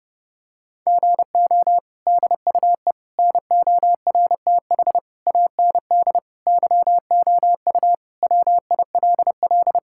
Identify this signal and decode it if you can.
Morse code, heard as a keyed tone, about 30 wpm, 700 Hz. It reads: GO DUE NORTH AND YOU WILL
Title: Sentences from Top 1000 Words - Review Part 6 - 30wpm